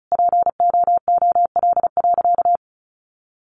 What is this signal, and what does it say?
Morse code, heard as a keyed tone, about 34 wpm, 700 Hz. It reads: POOL.
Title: Additional Easy Phrases 35wpm